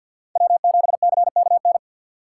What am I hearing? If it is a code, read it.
Morse code, heard as a keyed tone, about 51 wpm, 700 Hz. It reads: W7CKN
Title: USA Call Signs - 3x Character Spacing - 50wpm